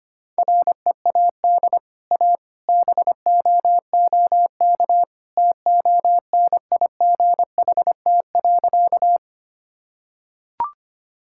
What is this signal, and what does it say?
Morse code, heard as a keyed tone, about 25 wpm, 700 Hz. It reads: READ A BOOK TONIGHT.
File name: Easy Sentences with Hints - 25wpm